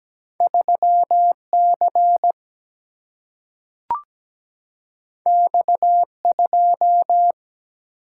Morse code, heard as a keyed tone, about 17 wpm, 700 Hz. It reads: 3C X2
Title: Single-Letter Single-Number 17wpm